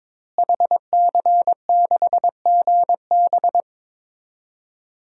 Morse code, heard as a keyed tone, about 22 wpm, 700 Hz. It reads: HC6GB
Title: DXCC-100 Call Signs - 5 Characters - Repeated 2x - 22wpm